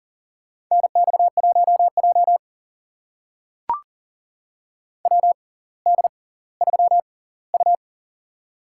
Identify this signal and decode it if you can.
Morse code, heard as a keyed tone, about 40 wpm, spaced as for 20 wpm, 700 Hz. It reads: NX1J WD3U
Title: USA Call Signs - 4 Characters - 6x Character Spacing - 40wpm